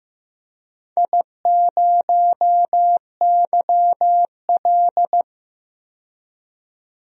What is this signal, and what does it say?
Morse code, heard as a keyed tone, about 15 wpm, 700 Hz. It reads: I0YL
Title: DXCC-100 Call Signs - 4 Characters - 15wpm